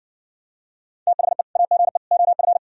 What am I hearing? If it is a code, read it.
Morse code, heard as a keyed tone, about 60 wpm, 700 Hz. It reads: THE AGE OF